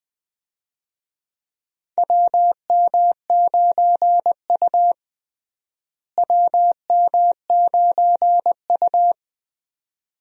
Morse code, heard as a keyed tone, about 20 wpm, 700 Hz. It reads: WM9U WM9U
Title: USA Call Signs - 4 Characters - Repeated 2x - 20wpm